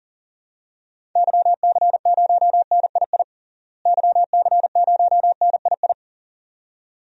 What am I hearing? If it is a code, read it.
Morse code, heard as a keyed tone, about 40 wpm, 700 Hz. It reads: YC0NII YC0NII
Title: DXCC-100 Call Signs - 6 Characters - Repeated 2x - 40wpm